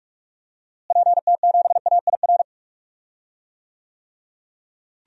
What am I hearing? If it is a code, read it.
Morse code, heard as a keyed tone, about 46 wpm, 700 Hz. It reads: PT7AIR
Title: DXCC-100 Call Signs - 6 Characters - Repeated 2x - 45wpm